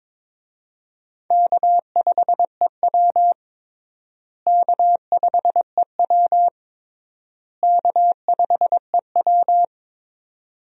Morse code, heard as a keyed tone, about 22 wpm, 700 Hz. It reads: K5EW K5EW K5EW
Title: USA Call Signs - 4 Characters - Repeated 3x - 22wpm